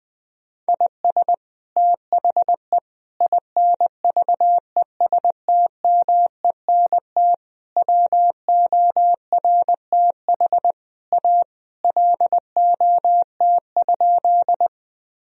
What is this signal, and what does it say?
Morse code, heard as a keyed tone, about 20 wpm, 700 Hz. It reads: IS THE INVESTMENT WORTH A LOT?
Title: Sentences from Top 1000 Words - Review Part 4 - 20wpm